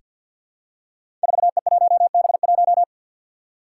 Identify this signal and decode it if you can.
Morse code, heard as a keyed tone, about 49 wpm, 700 Hz. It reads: 4E1B1
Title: Five Challenging Letter-Number Combinations 50wpm